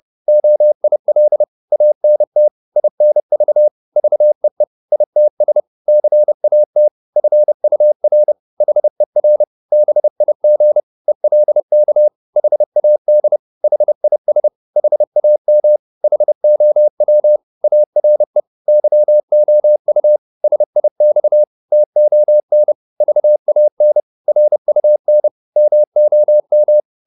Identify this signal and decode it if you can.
Morse code, heard as a keyed tone, about 30 wpm, 600 Hz. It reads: OIL ANT INV VEE ITS CAT FUR HER BIG ELK HAD HIS HAM HOW ARE YOU SIX TON VAN RUN MOM